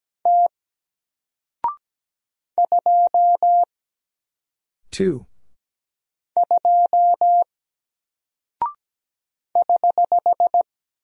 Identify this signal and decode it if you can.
Morse code, heard as a keyed tone, about 17 wpm, 700 Hz. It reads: T 2 2 <HH>
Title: Single Character (Letter:Number:Punctuation:Prosign) Alt 17wpm